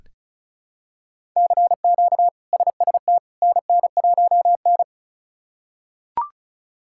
Morse code, heard as a keyed tone, about 34 wpm, 700 Hz. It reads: CQ SST NN1N
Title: K1USN SST QSO Practice - 35wpm